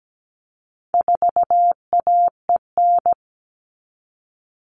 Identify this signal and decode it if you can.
Morse code, heard as a keyed tone, about 17 wpm, 700 Hz. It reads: 4AEN
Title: All Cut Numbers - 4-Digit 17wpm